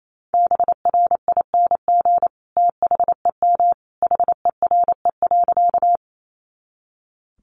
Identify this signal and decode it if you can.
Morse code, heard as a keyed tone, about 28 wpm, 700 Hz. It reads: BRING THEM HERE.
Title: Sentences from Top 500 Words 28wpm